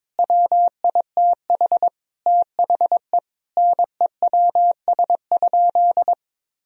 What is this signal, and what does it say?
Morse code, heard as a keyed tone, about 23 wpm, 700 Hz. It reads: WITH THE NEWS?
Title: Sentences from Top 1000 Words - Review Part 4 - 22wpm